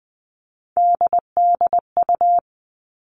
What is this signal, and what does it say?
Morse code, heard as a keyed tone, about 20 wpm, 700 Hz. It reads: DDU
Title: All Cut Numbers - 3-Digit 20wpm